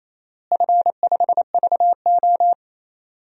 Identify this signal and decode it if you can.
Morse code, heard as a keyed tone, about 27 wpm, 700 Hz. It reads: F5VO
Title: DXCC-100 Call Signs - 4 Characters - Repeated 3x - 28wpm